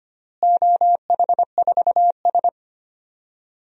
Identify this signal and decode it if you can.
Morse code, heard as a keyed tone, about 25 wpm, 700 Hz. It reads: OH4S
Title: DXCC-100 Call Signs - 4 Characters - Repeated 3x - 25wpm